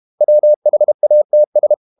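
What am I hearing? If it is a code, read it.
Morse code, heard as a keyed tone, about 32 wpm, 600 Hz. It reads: WHATS
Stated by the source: Morse code practice at 32 wpm from CW med Gnister, reading Ken Follett's novel Never in English.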